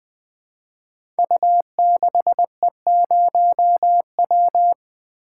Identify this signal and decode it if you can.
Morse code, heard as a keyed tone, about 20 wpm, 700 Hz. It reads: U6E0W
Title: Code Groups - 5 Letter:Numbers - 20wpm